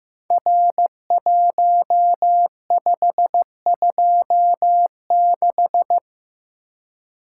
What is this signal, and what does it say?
Morse code, heard as a keyed tone, about 15 wpm, 700 Hz. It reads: R1526
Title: Code Groups - 5 Letter:Numbers - 15wpm